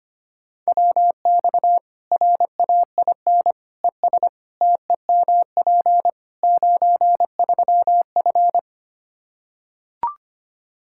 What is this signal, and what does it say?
Morse code, heard as a keyed tone, about 25 wpm, 700 Hz. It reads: WX RAIN ES TEMP 93F
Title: Practice QSO - Exchange 2 - Info - 25wpm